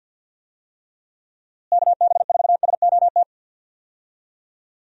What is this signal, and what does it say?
Morse code, heard as a keyed tone, about 52 wpm, 700 Hz. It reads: KB4SOT
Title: USA Call Signs - 6 Characters - Repeated 2x - 50wpm